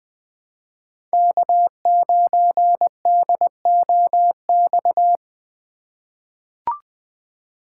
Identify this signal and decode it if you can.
Morse code, heard as a keyed tone, about 20 wpm, 700 Hz. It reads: K9DOX
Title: USA Call Signs - 5 Characters - 2x Character Spacing - 20wpm